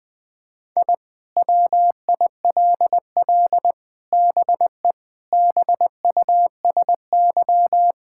Morse code, heard as a keyed tone, about 21 wpm, 700 Hz. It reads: I WILL BE BUSY
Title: Sentences with 5-Word Phrases - 20wpm